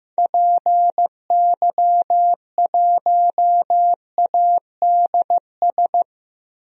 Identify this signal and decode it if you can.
Morse code, heard as a keyed tone, about 15 wpm, 700 Hz. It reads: PY1ADS
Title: DXCC-100 Call Signs - 6 Characters - 15wpm